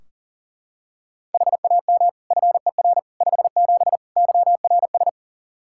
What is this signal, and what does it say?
Morse code, heard as a keyed tone, about 40 wpm, 700 Hz. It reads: HAM FER 57 YRS
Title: Practice QSO - Exchange 3 - More Info - 40wpm